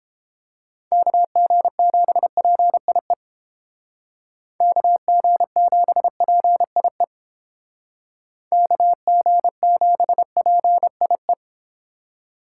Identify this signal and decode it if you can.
Morse code, heard as a keyed tone, about 32 wpm, 700 Hz. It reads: KG7PIE KG7PIE KG7PIE